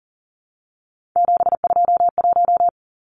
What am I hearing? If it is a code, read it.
Morse code, heard as a keyed tone, about 40 wpm, 700 Hz. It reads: 721
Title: Numbers - 3-Digit 40wpm